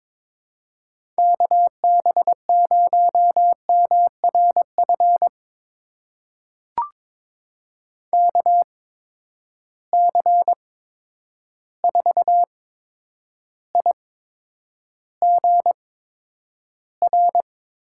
Morse code, heard as a keyed tone, about 22 wpm, spaced as for 9 wpm, 700 Hz. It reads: KB0MRF KC4IGR
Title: USA Call Signs - 6 Characters - 8x Character Spacing - 22wpm